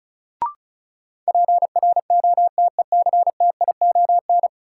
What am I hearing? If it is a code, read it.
Morse code, heard as a keyed tone, about 36 wpm, 700 Hz. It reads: PROTECTION
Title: Sentences from Top 1600 Words - New Word Drill Alt - 35wpm